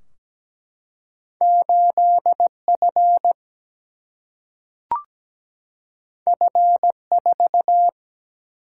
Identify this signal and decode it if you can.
Morse code, heard as a keyed tone, about 17 wpm, 700 Hz. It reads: 8 F F 4
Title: Single-Letter Single-Number 17wpm